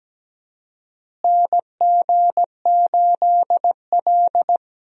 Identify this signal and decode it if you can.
Morse code, heard as a keyed tone, about 17 wpm, 700 Hz. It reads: NG8L
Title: USA Call Signs - 4 Characters - Repeated 3x - 17wpm